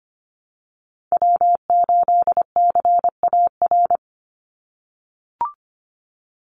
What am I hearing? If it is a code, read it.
Morse code, heard as a keyed tone, about 25 wpm, 700 Hz. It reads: W8CAR
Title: Call Signs - 25wpm